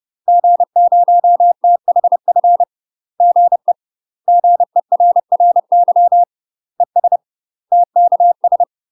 Morse code, heard as a keyed tone, about 30 wpm, 700 Hz. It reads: G0THF GE GERRY ES TKS
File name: UK MASTER QSOs TRAINING / 30WPM / UK MASTER QSOs 0005